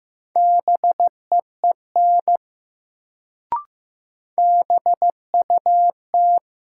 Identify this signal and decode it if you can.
Morse code, heard as a keyed tone, about 15 wpm, 700 Hz. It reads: BEEN BUT
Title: Sets of 2 Words - New Word Drill - 15wpm